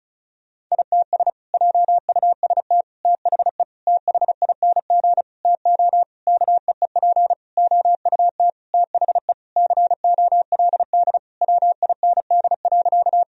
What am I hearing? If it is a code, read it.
Morse code, heard as a keyed tone, about 36 wpm, 700 Hz. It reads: ITS JUST THE THING TO KEEP OUT THE COLD WIND.
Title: Sentences from Top 1000 Words - Review Part 2 - 35wpm